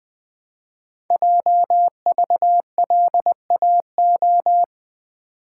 Morse code, heard as a keyed tone, about 20 wpm, 700 Hz. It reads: JVLAO
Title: Code Groups - 5 Letter:Numbers - 20wpm